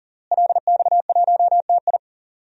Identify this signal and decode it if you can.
Morse code, heard as a keyed tone, about 40 wpm, 700 Hz. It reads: LX1TI
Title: DXCC-100 Call Signs - 5 Characters - Repeated 3x - 40wpm